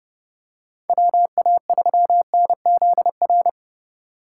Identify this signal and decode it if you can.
Morse code, heard as a keyed tone, about 30 wpm, 700 Hz. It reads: WA3NZR
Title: K1USN SST QSO Practice - 30wpm